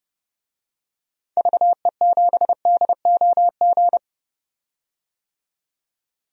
Morse code, heard as a keyed tone, about 31 wpm, 700 Hz. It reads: VE7DOG
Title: DXCC-100 Call Signs - 6 Characters - 30wpm